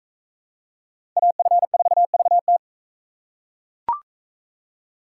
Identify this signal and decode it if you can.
Morse code, heard as a keyed tone, about 40 wpm, 700 Hz. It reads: AF4VT